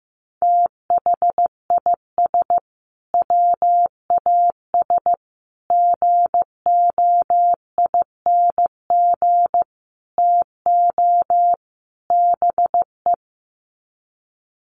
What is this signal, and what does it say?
Morse code, heard as a keyed tone, about 15 wpm, 700 Hz. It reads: THIS WAS GOING TO BE
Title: Sets of 5 Words Encore2 15wpm